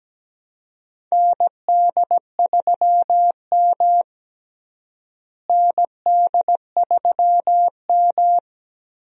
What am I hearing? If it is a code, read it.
Morse code, heard as a keyed tone, about 17 wpm, 700 Hz. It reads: ND3M ND3M
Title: USA Call Signs - 4 Characters - Repeated 3x - 17wpm